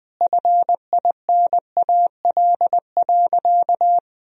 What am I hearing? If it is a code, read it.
Morse code, heard as a keyed tone, about 20 wpm, 700 Hz. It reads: FINAL.
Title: Sentences from Top 1000 Words - Review Part 2 - 20wpm